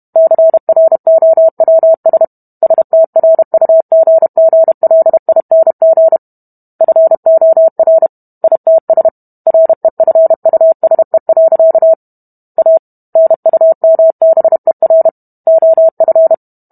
Morse code, heard as a keyed tone, about 32 wpm, 650 Hz. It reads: CROWS STRUGGLING FOR ITS REFUSE. A NUMBER OF